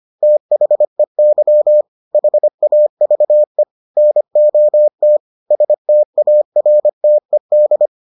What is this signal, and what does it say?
Morse code, heard as a keyed tone, about 25 wpm, 600 Hz. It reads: THEY HAVE NOT STARTED